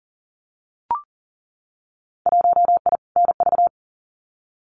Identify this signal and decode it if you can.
Morse code, heard as a keyed tone, about 40 wpm, 700 Hz. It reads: 1I NV
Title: Winter Field Day Exchanges Encore 40wpm